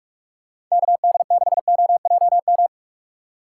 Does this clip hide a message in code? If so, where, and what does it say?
Yes: Morse code from 0.7 to 2.7 s, heard as a keyed tone, about 44 wpm, 700 Hz. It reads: KD6OJM